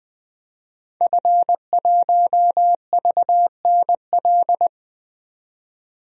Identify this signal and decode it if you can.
Morse code, heard as a keyed tone, about 20 wpm, 700 Hz. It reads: F1VNL
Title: DXCC-100 Call Signs - 5 Characters - 20wpm